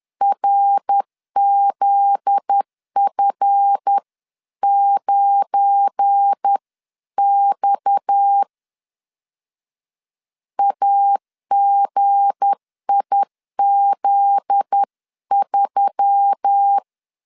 Morse code, heard as a keyed tone, about 10 wpm, 800 Hz. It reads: RZF9X AGIZ3
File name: LEZIONI / 7 GF9Z / 7 GF9Z 10wpm 2611